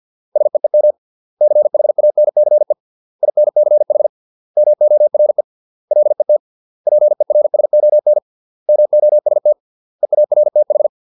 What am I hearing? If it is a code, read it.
Morse code, heard as a keyed tone, about 50 wpm, 600 Hz. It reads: SEEM CHANCE INCH MORE LET PERSON MOST EARTH